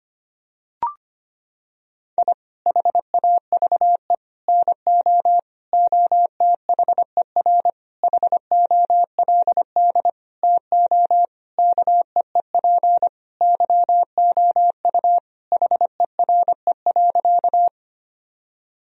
Morse code, heard as a keyed tone, about 25 wpm, 700 Hz. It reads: I HAVE NO OTHER HOLD TO KEEP YOU HERE.
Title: Sentences from Top 1000 Words - Review Part 2 - 25wpm